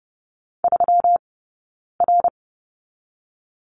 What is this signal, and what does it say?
Morse code, heard as a keyed tone, about 30 wpm, spaced as for 9 wpm, 700 Hz. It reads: 3R
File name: Lesson 20d - Number 3 - Two Characters 30wpm